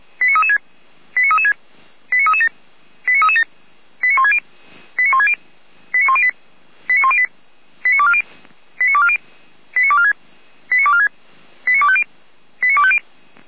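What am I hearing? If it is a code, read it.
Five-tone selective call (ZVEI), heard as quick runs of five tones, tones of about 70 ms. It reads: 89336 89336 89337 89337 89160 89160 89108 89108 89240 89240 89246 89246 89250 89250
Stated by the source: ZVEI five-tone selective call received on an SDR.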